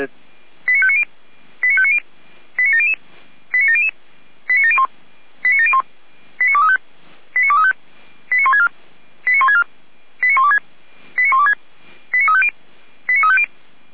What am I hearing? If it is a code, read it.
Five-tone selective call (ZVEI), heard as quick runs of five tones, tones of about 70 ms, with speech. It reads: 89590 89590 89700 89700 89701 89701 89235 89235 89164 89164 89126 89126 89350 89350